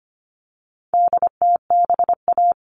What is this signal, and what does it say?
Morse code, heard as a keyed tone, about 25 wpm, 700 Hz. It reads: DTBA
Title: All Cut Numbers - 4-Digit 25wpm